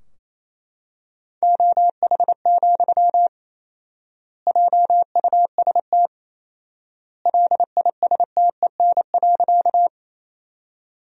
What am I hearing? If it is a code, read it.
Morse code, heard as a keyed tone, about 28 wpm, 700 Hz. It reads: OH, JUST LISTEN.
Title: 4x Word-Spacing - Sentences from Top 500 Words - 28wpm